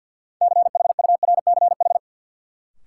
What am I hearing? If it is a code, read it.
Morse code, heard as a keyed tone, about 50 wpm, 700 Hz. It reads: CHURCH
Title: Sets of 5 Words - New Word Drill - 50wpm